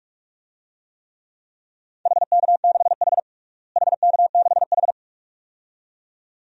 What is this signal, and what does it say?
Morse code, heard as a keyed tone, about 46 wpm, 700 Hz. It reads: HK6H HK6H
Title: DXCC-100 Call Signs - 4 Characters - Repeated 2x - 45wpm